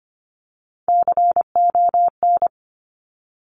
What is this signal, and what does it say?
Morse code, heard as a keyed tone, about 25 wpm, 700 Hz. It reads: CON
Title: Top Three Letter Combinations in Words 25wpm